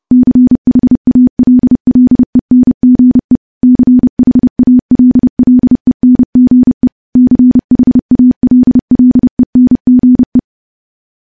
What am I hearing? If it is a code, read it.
Morse code, heard as a keyed tone, about 30 wpm, 250 Hz. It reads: CHALLENGE CHALLENGE CHALLENGE
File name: morse / morse-training-30WPM